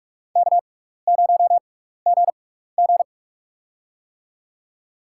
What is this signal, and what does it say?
Morse code, heard as a keyed tone, about 44 wpm, spaced as for 15 wpm, 700 Hz. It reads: K0GG
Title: USA Call Signs - 4 Characters - 6x Character Spacing - 45wpm